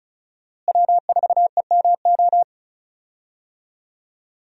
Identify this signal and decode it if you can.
Morse code, heard as a keyed tone, about 36 wpm, 700 Hz. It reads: W4EMO
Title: USA Call Signs - 5 Characters - Repeated 2x - 35wpm